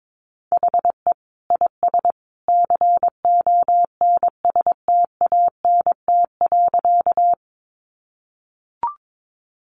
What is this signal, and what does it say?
Morse code, heard as a keyed tone, about 22 wpm, 700 Hz. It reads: HE IS CONSTANT.
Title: Sentences from Top 1500 Words 22wpm